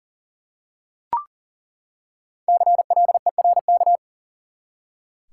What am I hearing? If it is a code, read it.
Morse code, heard as a keyed tone, about 40 wpm, 700 Hz. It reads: CLERK
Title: Five Letter Words - 40wpm